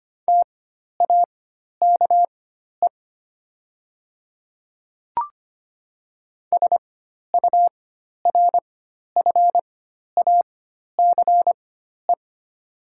Morse code, heard as a keyed tone, about 25 wpm, spaced as for 12 wpm, 700 Hz. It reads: TAKE SURFACE
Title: 4x Character Spacing - Top 500 Words - 25wpm